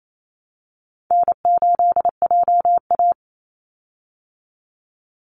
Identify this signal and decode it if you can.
Morse code, heard as a keyed tone, about 28 wpm, 700 Hz. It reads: N8JA
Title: Call Signs Repeated 2x - 28wpm